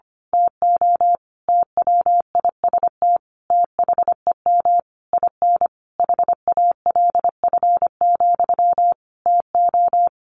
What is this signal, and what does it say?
Morse code, heard as a keyed tone, about 25 wpm, 700 Hz. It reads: TO TWIST THEM IN HALF, TO
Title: A Sound Of Thunder 25wpm